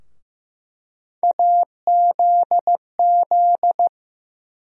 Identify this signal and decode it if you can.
Morse code, heard as a keyed tone, about 15 wpm, 700 Hz. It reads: AZZ
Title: USA Call Sign Postfixes - 3 Letters 15wpm